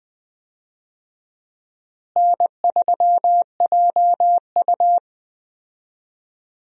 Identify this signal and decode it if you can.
Morse code, heard as a keyed tone, about 20 wpm, 700 Hz. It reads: N3JU
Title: USA Call Signs - 4 Characters - Repeated 3x - 20wpm